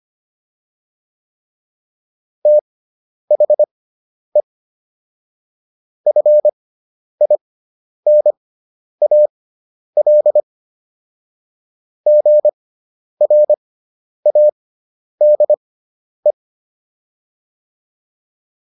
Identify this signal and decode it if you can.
Morse code, heard as a keyed tone, about 25 wpm, spaced as for 10 wpm, 600 Hz. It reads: THE FINAL GRADE